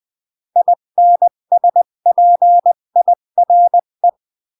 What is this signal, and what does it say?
Morse code, heard as a keyed tone, about 20 wpm, 700 Hz. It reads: INSPIRE
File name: CW / jodorowsky / 82 commandments 20-15